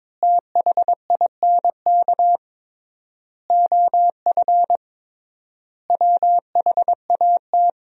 Morse code, heard as a keyed tone, about 22 wpm, 700 Hz. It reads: THINK OF WHAT
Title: 3x Word-Spacing - Sentences from Top 400 Words - 22wpm